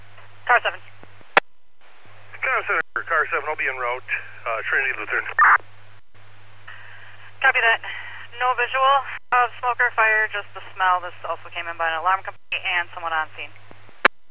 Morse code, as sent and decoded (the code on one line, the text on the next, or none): none